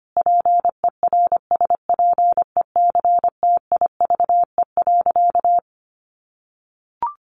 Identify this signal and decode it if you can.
Morse code, heard as a keyed tone, about 25 wpm, 700 Hz. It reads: PERSPECTIVE.
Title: Sentences from Top 1200 Words 25wpm